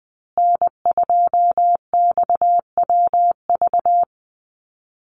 Morse code, heard as a keyed tone, about 20 wpm, 700 Hz. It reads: N2XWV
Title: USA Call Signs Repeated 3x 20wpm